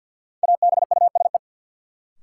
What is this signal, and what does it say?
Morse code, heard as a keyed tone, about 52 wpm, 700 Hz. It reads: ABUSE
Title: Sentences from Top 1300 Words - New Word Drill Alt - 50wpm